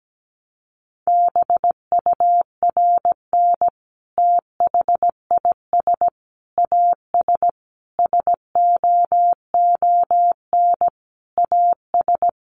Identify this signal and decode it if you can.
Morse code, heard as a keyed tone, about 17 wpm, 700 Hz. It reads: BURN THIS AS SOON AS
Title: Sentences from Top 1300 Words 17wpm